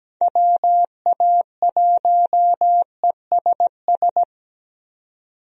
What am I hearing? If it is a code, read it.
Morse code, heard as a keyed tone, about 17 wpm, 700 Hz. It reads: WA1ESS
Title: USA Call Signs - 6 Characters - 17wpm